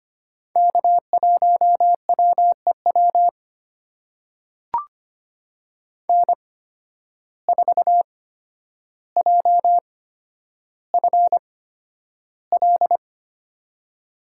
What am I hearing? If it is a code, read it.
Morse code, heard as a keyed tone, about 25 wpm, spaced as for 11 wpm, 700 Hz. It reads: K1WEW N4JFL
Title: USA Call Signs - 5 Characters - 8x Character Spacing - 25wpm